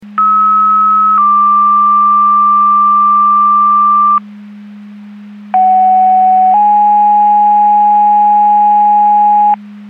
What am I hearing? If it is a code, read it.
Two-tone page, heard as a one-second tone, then a three-second tone, about 1290 then 1180 Hz, about 750 then 830 Hz. It is heard from 0.2 to 9.5 s.
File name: Portage County 154.3850 Fire EMS Paging